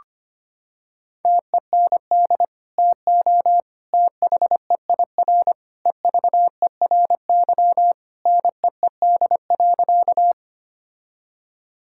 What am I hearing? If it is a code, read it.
Morse code, heard as a keyed tone, about 25 wpm, 700 Hz. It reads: TEND TO THEIR EVERY NEED.
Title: Sentences from Top 1000 Words - Review Part 5 - 25wpm